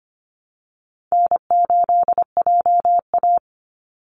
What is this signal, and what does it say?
Morse code, heard as a keyed tone, about 25 wpm, 700 Hz. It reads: N8JA
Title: Easy Call Signs Repeated 3x - 25wpm